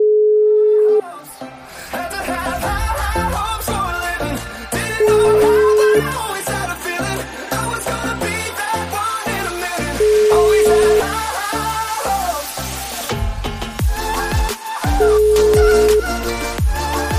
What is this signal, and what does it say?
Ringback tone (Europe, 425 Hz), with noise.